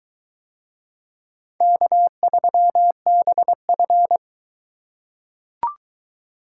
Morse code, heard as a keyed tone, about 22 wpm, 700 Hz. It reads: K3BF